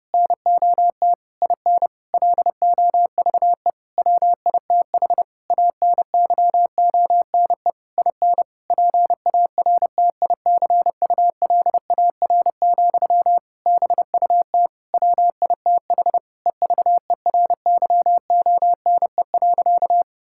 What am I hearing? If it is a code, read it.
Morse code, heard as a keyed tone, about 32 wpm, 700 Hz. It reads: NOT IN LOVE WITH ANYONE IN PARTICULAR, BUT WITH EVERYONE.
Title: Sentences from Top 1000 Words - Review Part 5 - 30wpm